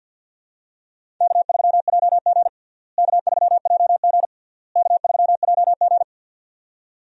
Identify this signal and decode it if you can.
Morse code, heard as a keyed tone, about 49 wpm, 700 Hz. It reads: K3JG K3JG K3JG
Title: USA Call Signs - 4 Characters - Repeated 3x - 50wpm